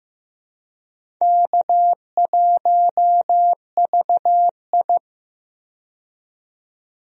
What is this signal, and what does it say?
Morse code, heard as a keyed tone, about 15 wpm, 700 Hz. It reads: K1VI
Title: USA Call Signs - 4 Characters - 15wpm